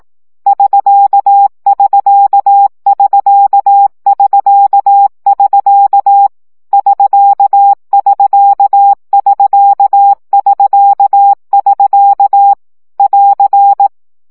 Morse code, heard as a keyed tone, about 18 wpm, 800 Hz. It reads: <SK><SK><SK><SK><SK> <SK><SK><SK><SK><SK> <AR>
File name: CW gyakorlas sk 15 baud